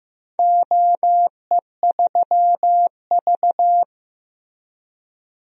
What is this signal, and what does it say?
Morse code, heard as a keyed tone, about 15 wpm, 700 Hz. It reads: OE3V
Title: DXCC-100 Call Signs - 4 Characters - Repeated 3x - 15wpm